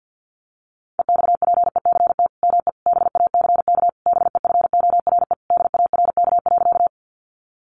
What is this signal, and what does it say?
Morse code, heard as a keyed tone, about 50 wpm, 700 Hz. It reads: EXPECT ME BACK BEFORE DARK.
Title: Sentences from Top 1100 Words 50wpm